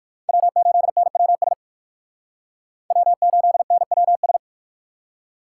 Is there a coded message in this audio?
Yes: Morse code from 0.3 to 4.4 s, heard as a keyed tone, about 63 wpm, 700 Hz. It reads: W8NWS W8NWS